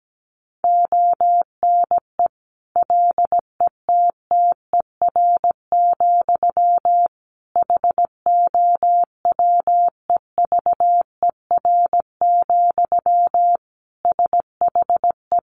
Morse code, heard as a keyed tone, about 17 wpm, 700 Hz. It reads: ONE LETTER, HOWEVER, SHE
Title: Sentences from Top 1200 Words 17wpm